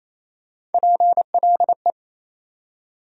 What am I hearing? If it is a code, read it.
Morse code, heard as a keyed tone, about 28 wpm, 700 Hz. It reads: PLE
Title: USA Call Sign Postfixes - 3 Letters 28wpm